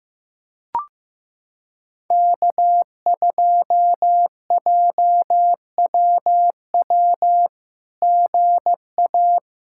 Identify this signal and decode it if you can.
Morse code, heard as a keyed tone, about 15 wpm, 700 Hz. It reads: K2JWW GA
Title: SOTA QSO Practice 15wpm